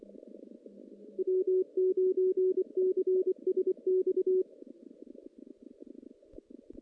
Morse code, heard as a keyed tone, about 25 wpm, 440 Hz. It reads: W9CSX
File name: 2016 JAN CW NAQP / 2016 Jan naqp k5ot - 0-5